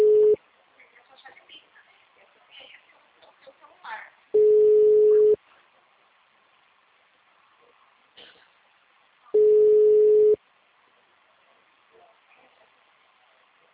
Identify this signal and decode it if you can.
Ringback tone (Europe, 425 Hz), with speech.
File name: out- -2004- - - .143